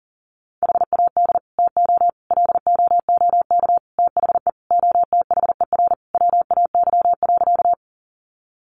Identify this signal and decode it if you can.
Morse code, heard as a keyed tone, about 40 wpm, 700 Hz. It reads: HAD TO LOOK THE OTHER WAY.
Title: Sentences from Top 100 Words 40wpm